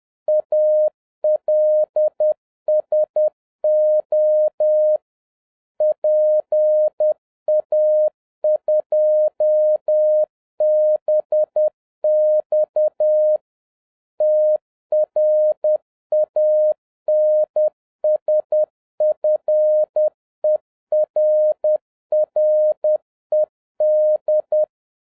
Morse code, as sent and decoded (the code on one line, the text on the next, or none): .- .-.. ... --- / .--. .- ..--- -... -..- / - .-. .- -. ... ..-. .-. .-. -..
ALSO PA2BX TRANSFERRED